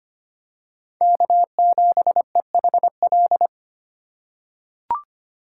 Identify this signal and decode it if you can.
Morse code, heard as a keyed tone, about 25 wpm, 700 Hz. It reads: K7EHL
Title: USA Call Signs - 8x Character Spacing - 25wpm